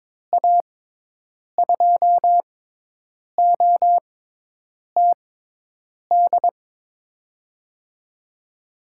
Morse code, heard as a keyed tone, about 22 wpm, spaced as for 8 wpm, 700 Hz. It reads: A2OTD
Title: USA Call Signs - 6 Characters - 6x Character Spacing - 22wpm